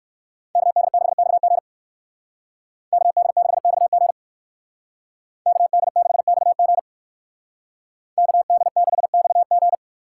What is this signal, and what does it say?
Morse code, heard as a keyed tone, about 60 wpm, 700 Hz. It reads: KD6XG KD6XG KD6XG KD6XG